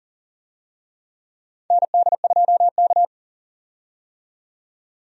Morse code, heard as a keyed tone, about 40 wpm, 700 Hz. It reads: ND2K
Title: USA Call Signs - 4 Characters - 40wpm